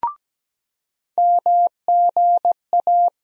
Morse code, heard as a keyed tone, about 17 wpm, 700 Hz. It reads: MGA
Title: USA Call Sign Postfixes - 3 Letters 17wpm